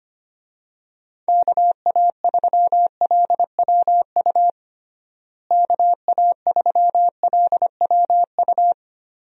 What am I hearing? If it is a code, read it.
Morse code, heard as a keyed tone, about 25 wpm, 700 Hz. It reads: KA3LWU KA3LWU
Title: USA Call Signs - 6 Characters - Repeated 2x - 25wpm